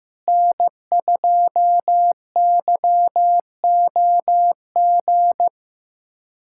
Morse code, heard as a keyed tone, about 15 wpm, 700 Hz. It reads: N2YOG
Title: USA Callsign DE Callsign 15wpm